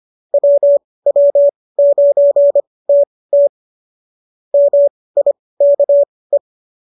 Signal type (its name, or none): Morse code